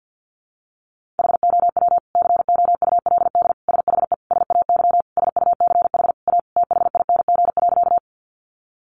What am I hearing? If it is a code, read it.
Morse code, heard as a keyed tone, about 50 wpm, 700 Hz. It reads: HOW COULD SHE SAY SUCH A THING.
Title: Sentences from Top 200 Words 50wpm